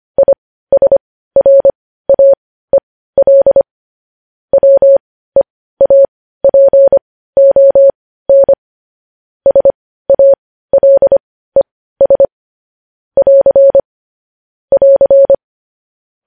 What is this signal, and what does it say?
Morse code, heard as a keyed tone, about 25 wpm, spaced as for 15 wpm, 550 Hz. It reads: ISRAEL WEAPON SALES <AR> <AR>